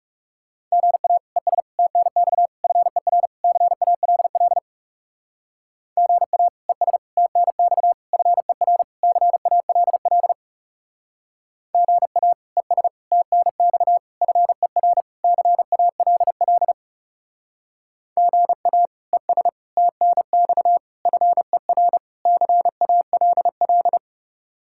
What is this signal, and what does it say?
Morse code, heard as a keyed tone, about 36 wpm, 700 Hz. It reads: GA ES TNX FER CALL GA ES TNX FER CALL GA ES TNX FER CALL GA ES TNX FER CALL
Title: Speed-Racing Common QSO Phrases - 30wpm